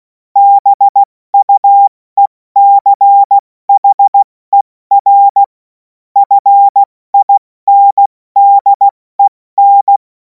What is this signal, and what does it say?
Morse code, heard as a keyed tone, about 16 wpm, 800 Hz. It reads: BUECHER FINDEN
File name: Morsekurs / Witze / Witz 24. März 2026 70 F80